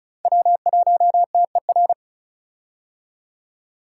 Morse code, heard as a keyed tone, about 36 wpm, 700 Hz. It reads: W1TER
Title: K1USN SST QSO Practice - 35wpm